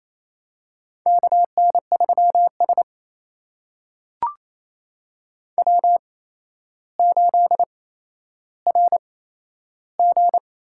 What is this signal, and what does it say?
Morse code, heard as a keyed tone, about 28 wpm, spaced as for 12 wpm, 700 Hz. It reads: KN3S W8RG
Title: USA Call Signs - 4 Characters - 8x Character Spacing - 28wpm